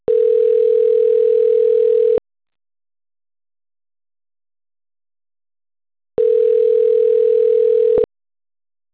Ringback tone (North America), 440 + 480 Hz.